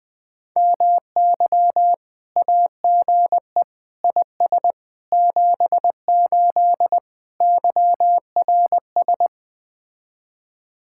Morse code, heard as a keyed tone, about 20 wpm, 700 Hz. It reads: MY AGE IS 78 YRS
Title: Speed-Racing Common QSO Phrases Encore - 17wpm